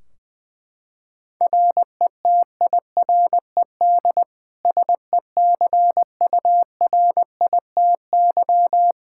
Morse code, heard as a keyed tone, about 21 wpm, 700 Hz. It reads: RETIRED SECURITY
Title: K1USN SST QSO Practice 20wpm